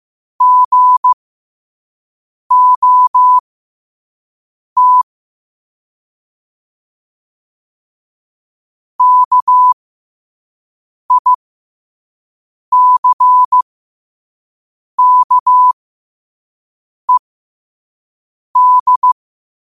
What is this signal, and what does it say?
Morse code, heard as a keyed tone, about 15 wpm, spaced as for 5 wpm, 1000 Hz. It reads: GOT KICKED